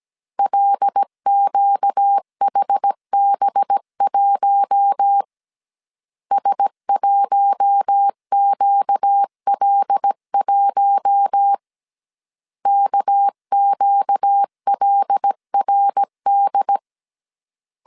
Morse code, heard as a keyed tone, about 17 wpm, 800 Hz. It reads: LQHB1 S1QL1 KQLRD
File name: CW / DET / 1 6 1500C S B 16wpm 2619